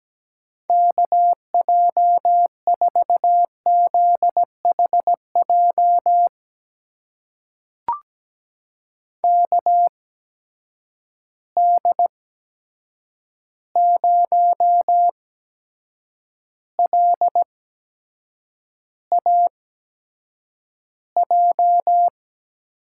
Morse code, heard as a keyed tone, about 17 wpm, spaced as for 7 wpm, 700 Hz. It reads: KJ4ZHJ KD0LAJ
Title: USA Call Signs - 6 Characters - 8x Character Spacing - 17wpm